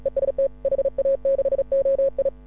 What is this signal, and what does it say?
Morse code, heard as a keyed tone, about 35 wpm, 530 Hz. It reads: EST HA6OI
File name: AUDIO2018 / HA6OI CW